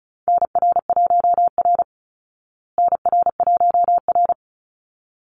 Morse code, heard as a keyed tone, about 34 wpm, 700 Hz. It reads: NR1R NR1R
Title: Call Signs Repeated 3x - 35wpm